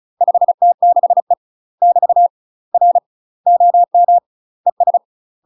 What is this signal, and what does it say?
Morse code, heard as a keyed tone, about 34 wpm, 700 Hz. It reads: 5T6E <BT> R OM ES
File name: UK MASTER QSOs TRAINING/35WPM